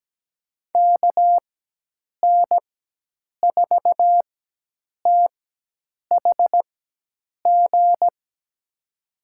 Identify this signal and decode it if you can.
Morse code, heard as a keyed tone, about 17 wpm, spaced as for 8 wpm, 700 Hz. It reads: KN4THG
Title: USA Call Signs - 6 Characters - 4x Character Spacing - 17wpm